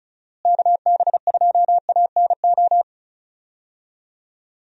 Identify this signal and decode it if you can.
Morse code, heard as a keyed tone, about 34 wpm, 700 Hz. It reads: KB2ANO
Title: USA Call Signs - 6 Characters - Repeated 3x - 35wpm